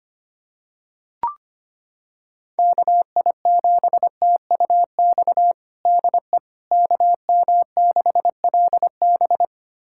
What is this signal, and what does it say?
Morse code, heard as a keyed tone, about 25 wpm, 700 Hz. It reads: KI7TUX DE KM6LB
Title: USA Callsign DE Callsign 25wpm